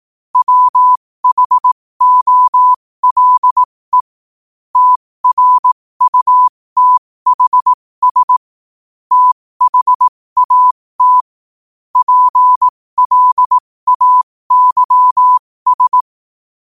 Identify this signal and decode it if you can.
Morse code, heard as a keyed tone, about 18 wpm, spaced as for 15 wpm, 1000 Hz. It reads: WHOLE TRUTHS THAT PLAYS
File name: qotdfiles / 15wpm